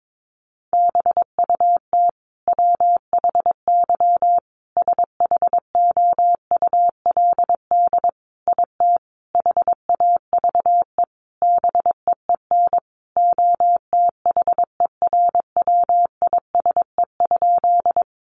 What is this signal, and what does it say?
Morse code, heard as a keyed tone, about 22 wpm, 700 Hz. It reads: BUT WHY SHOULD IT HAVE BEEN OTHERWISE?
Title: Sentences from Top 1200 Words 22wpm